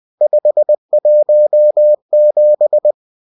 Morse code, heard as a keyed tone, about 20 wpm, 600 Hz. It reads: 517